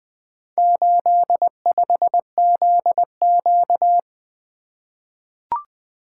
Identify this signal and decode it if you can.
Morse code, heard as a keyed tone, about 20 wpm, 700 Hz. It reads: 85ZQ
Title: Four Letter-Number Combinations - 20wpm